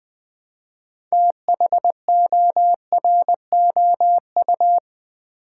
Morse code, heard as a keyed tone, about 20 wpm, 700 Hz. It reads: THOROU
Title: Build A Word - 3 Letters At A Time - 8-Letter Words - 20wpm